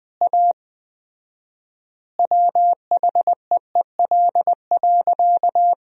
Morse code, heard as a keyed tone, about 21 wpm, 700 Hz. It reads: A WHEEL.
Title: 4x Word-Spacing - Sentences from Top 500 Words - 20wpm